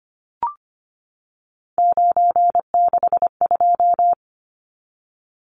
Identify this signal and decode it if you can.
Morse code, heard as a keyed tone, about 25 wpm, 690 Hz. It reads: 962
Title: Numbers - 3-Digit 25wpm